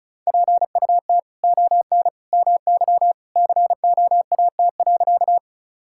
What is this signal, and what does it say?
Morse code, heard as a keyed tone, about 34 wpm, 700 Hz. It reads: PUT ON MY COAT.
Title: Sentences from Top 1600 Words 35wpm